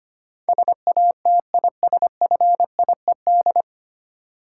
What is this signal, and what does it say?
Morse code, heard as a keyed tone, about 25 wpm, 700 Hz. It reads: SATISFIED
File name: practice / 25wpm / 9 Letter Words - First 3 Letters Given - 25wpm